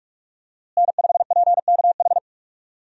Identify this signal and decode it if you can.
Morse code, heard as a keyed tone, about 46 wpm, 700 Hz. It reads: N5PKH
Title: USA Call Signs - 5 Characters - Repeated 2x - 45wpm